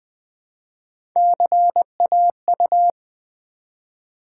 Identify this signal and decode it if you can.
Morse code, heard as a keyed tone, about 20 wpm, 700 Hz. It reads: CAU
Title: Build A Word - 1 Letter At A Time - 3 to 5-Letter Words - 20wpm